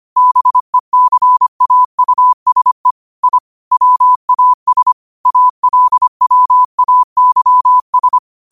Morse code, heard as a keyed tone, about 25 wpm, 1000 Hz. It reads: BECAUSE I WAS ALWAYS
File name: 25wpm